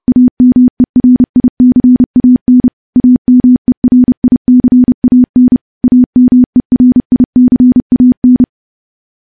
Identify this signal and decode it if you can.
Morse code, heard as a keyed tone, about 28 wpm, 250 Hz. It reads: AMERICAN AMERICAN AMERICAN
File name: morse / morse-training-30WPM